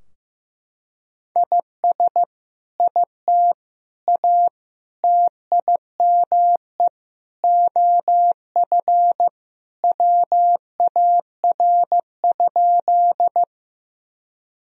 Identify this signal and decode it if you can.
Morse code, heard as a keyed tone, about 15 wpm, 700 Hz. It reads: IS IT A TIME OF WAR?
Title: Sentences from Top 1000 Words - Review Part 2 - 15wpm